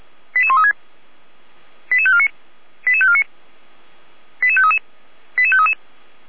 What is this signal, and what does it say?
Five-tone selective call (ZVEI), heard as quick runs of five tones, tones of about 70 ms. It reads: 88136 88549 88549 88533 88533